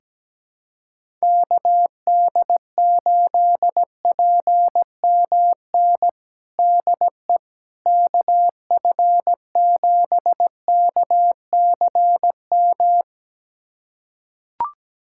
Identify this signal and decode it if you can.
Morse code, heard as a keyed tone, about 17 wpm, 700 Hz. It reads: KD8PMN DE KF7KCM
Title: USA Callsign DE Callsign - 6 Characters - 17wpm